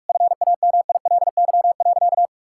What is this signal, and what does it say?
Morse code, heard as a keyed tone, about 46 wpm, 700 Hz. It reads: FAMILY.
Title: Sentences from Top 1000 Words - Review Part 5 - 45wpm